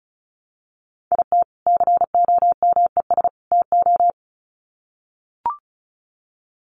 Morse code, heard as a keyed tone, about 34 wpm, 700 Hz. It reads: IT COMES TO
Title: Sets of 3 Words 35wpm